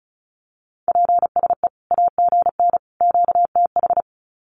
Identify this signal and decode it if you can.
Morse code, heard as a keyed tone, about 34 wpm, 700 Hz. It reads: PSE AGN QTH
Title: Common QSO Phrases 35wpm